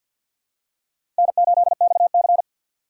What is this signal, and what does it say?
Morse code, heard as a keyed tone, about 47 wpm, 700 Hz. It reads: N8XC
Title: USA Call Signs - 4 Characters - 6x Character Spacing - 50wpm